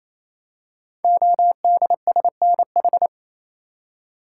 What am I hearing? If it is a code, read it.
Morse code, heard as a keyed tone, about 27 wpm, 700 Hz. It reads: ODSNH
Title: Code Groups - 5 Letter:Numbers - 28wpm